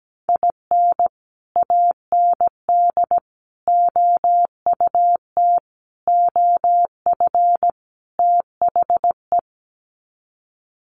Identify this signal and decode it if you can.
Morse code, heard as a keyed tone, about 17 wpm, 700 Hz. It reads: IN AND OUT OF THE
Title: Sets of 5 Words 17wpm